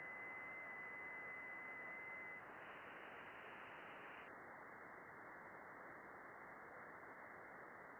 FT8.
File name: contests / june-vhf-k5tr- -0200